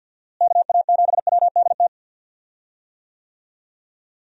Morse code, heard as a keyed tone, about 50 wpm, 700 Hz. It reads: KA7WDT
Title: USA Call Signs - 6 Characters - Repeated 3x - 50wpm